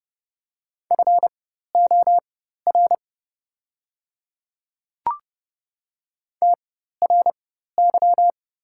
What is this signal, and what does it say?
Morse code, heard as a keyed tone, about 30 wpm, spaced as for 14 wpm, 700 Hz. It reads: FOR TRY
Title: 4x Character Spacing - Top 300 Words - 30wpm